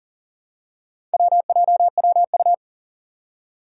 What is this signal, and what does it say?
Morse code, heard as a keyed tone, about 40 wpm, 700 Hz. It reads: WJWU
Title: Four Challenging Letter-Number Combinations 40wpm